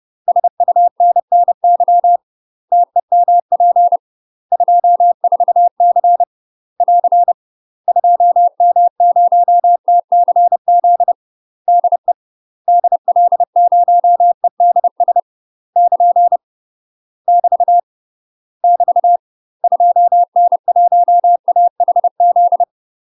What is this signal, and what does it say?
Morse code, heard as a keyed tone, about 30 wpm, 700 Hz. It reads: SUNNY TEMP 24C <AR> 2M0TCZ DE DL0EDS <KN> <BT> <BT> 2N1AHZ
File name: UK MASTER QSOs TRAINING/30WPM